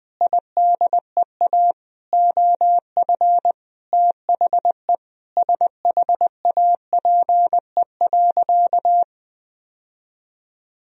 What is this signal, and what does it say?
Morse code, heard as a keyed tone, about 20 wpm, 700 Hz. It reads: IDEA OF THE SHAPE.
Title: Sentences from Top 1000 Words - Review Part 3 - 20wpm